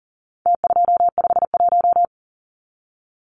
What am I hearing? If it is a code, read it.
Morse code, heard as a keyed tone, about 40 wpm, 700 Hz. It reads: T251
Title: Numbers - Contest Style - 4-Digit 40wpm